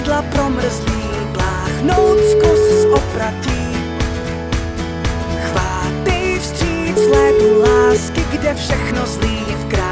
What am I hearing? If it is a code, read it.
Ringback tone (Europe, 425 Hz), with noise.